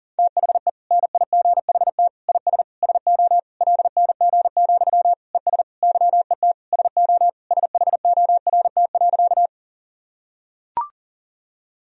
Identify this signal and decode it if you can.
Morse code, heard as a keyed tone, about 40 wpm, 700 Hz. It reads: THE NIGHT IS SO LNG, ES YET SO SHORT.
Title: Sentences from Top 500 Words and Abbreviations 40wpm